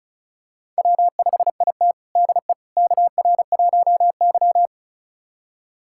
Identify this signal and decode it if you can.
Morse code, heard as a keyed tone, about 36 wpm, 700 Hz. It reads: W5IT DE KR1Y
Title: USA Callsign DE Callsign - 4 Characters - 35wpm